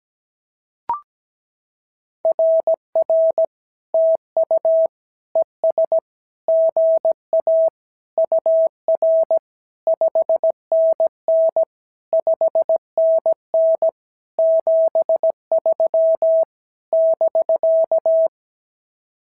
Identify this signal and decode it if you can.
Morse code, heard as a keyed tone, about 17 wpm, 650 Hz. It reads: RR TU ES GA UR 5NN 5NN 73 <BK>
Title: SOTA QSO Practice 17wpm